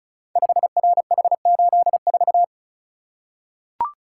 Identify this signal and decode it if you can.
Morse code, heard as a keyed tone, about 36 wpm, 690 Hz. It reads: 5RH84
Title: Code Groups - 5 Letter:Numbers - 35wpm